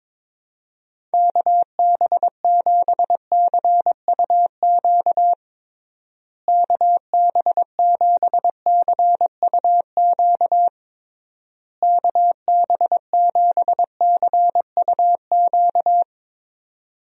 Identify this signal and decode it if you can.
Morse code, heard as a keyed tone, about 22 wpm, 700 Hz. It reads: KB7CUQ KB7CUQ KB7CUQ
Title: USA Call Signs - 6 Characters - Repeated 3x - 22wpm